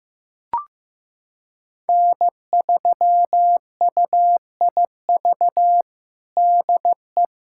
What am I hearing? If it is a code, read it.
Morse code, heard as a keyed tone, about 15 wpm, 700 Hz. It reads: N3UIV DE
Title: USA Callsign DE Callsign - 5 Characters - 15wpm